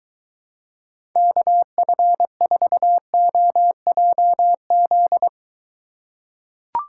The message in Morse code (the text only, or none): KF4OJZ